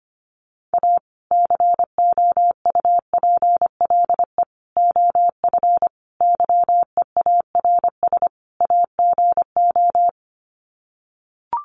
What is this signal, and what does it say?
Morse code, heard as a keyed tone, about 25 wpm, 700 Hz. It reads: A COUPLE OF YEARS AGO
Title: Sets of 5 Words 25wpm